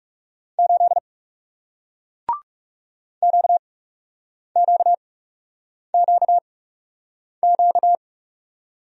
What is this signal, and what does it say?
Morse code, heard as a keyed tone, about 40 wpm, spaced as for 8 wpm, 700 Hz. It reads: 8 QQQQ